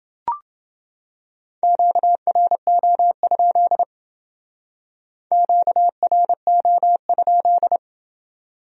Morse code, heard as a keyed tone, about 28 wpm, 700 Hz. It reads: QRO? QRO?